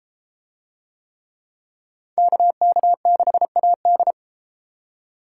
Morse code, heard as a keyed tone, about 29 wpm, 700 Hz. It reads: KK6AD